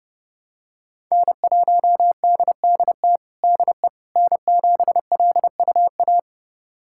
Morse code, heard as a keyed tone, about 30 wpm, 700 Hz. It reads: N1DDT DE N7LUA
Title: USA Callsign DE Callsign - 5 Characters - 30wpm